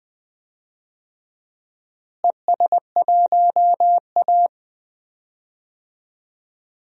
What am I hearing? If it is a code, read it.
Morse code, heard as a keyed tone, about 20 wpm, 700 Hz. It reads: ES1A
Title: DXCC-100 Call Signs - 4 Characters - Repeated 3x - 20wpm